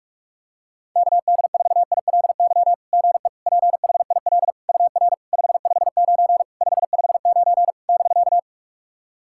Morse code, heard as a keyed tone, about 44 wpm, 690 Hz. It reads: KD4ILY GE PHIL UR 559 559 <BK>
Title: SOTA QSO Practice 45wpm